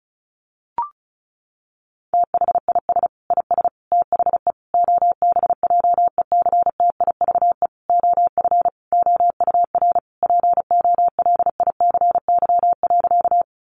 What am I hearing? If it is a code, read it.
Morse code, heard as a keyed tone, about 34 wpm, 700 Hz. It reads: THIS IS THE OBJECTIVE OF OUR POLICY.
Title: Sentences from Top 1200 Words 35wpm